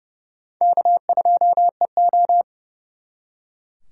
Morse code, heard as a keyed tone, about 36 wpm, 700 Hz. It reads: K2EO
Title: Speed Racing - USA 1x2 Call Signs 30wpm